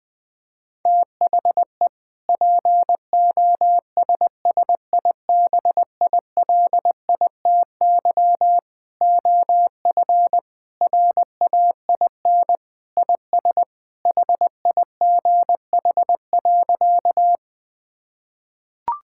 Morse code, heard as a keyed tone, about 20 wpm, 700 Hz. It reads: THE POSSIBILITY OF RAIN IS HIGH.
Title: Sentences with 3-Word Phrases - Spoken Beginning - 20wpm